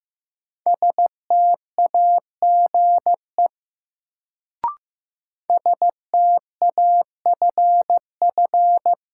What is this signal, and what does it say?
Morse code, heard as a keyed tone, about 15 wpm, 700 Hz. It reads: STAGE STAFF
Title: Sentences from Top 1200 Words - New Word Drill Alt - 15wpm